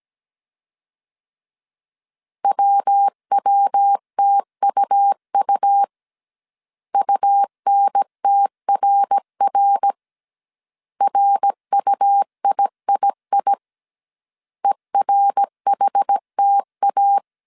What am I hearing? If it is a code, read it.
Morse code, heard as a keyed tone, about 17 wpm, 800 Hz. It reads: WWTUU UNTRR RUIII ERHTA